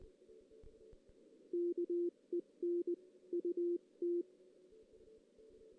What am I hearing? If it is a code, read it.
Morse code, heard as a keyed tone, about 19 wpm, 350 Hz. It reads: KEN UT
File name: 2016 JAN CW NAQP / 2016 Jan naqp k5ot - 0-3